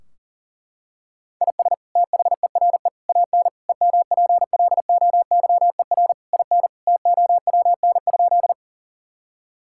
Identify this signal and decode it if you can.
Morse code, heard as a keyed tone, about 40 wpm, 700 Hz. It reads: IS THERE AN EMPLOYER IN TOWN?
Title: Sentences from Top 1600 Words 40wpm